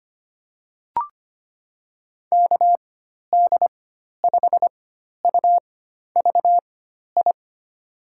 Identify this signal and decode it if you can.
Morse code, heard as a keyed tone, about 25 wpm, spaced as for 12 wpm, 700 Hz. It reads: KD5UVI
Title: USA Call Signs - 4x Character Spacing - 25wpm